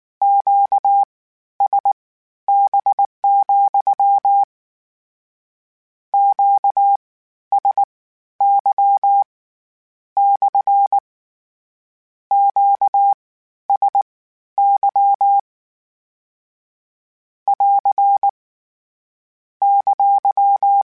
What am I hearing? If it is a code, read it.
Morse code, heard as a keyed tone, about 19 wpm, spaced as for 11 wpm, 800 Hz. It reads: QSB, QSY / QSY <AR> !
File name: Q Calls / Q Calls-mp3s / Q Calls SPSL2 @18